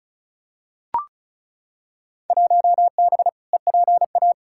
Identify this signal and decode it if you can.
Morse code, heard as a keyed tone, about 36 wpm, 700 Hz. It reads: 1B EPA
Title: Field Day Exchanges Encore - 35wpm